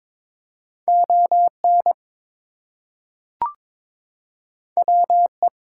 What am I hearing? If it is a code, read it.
Morse code, heard as a keyed tone, about 22 wpm, 700 Hz. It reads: ON WE
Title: Two Letter Words - 22wpm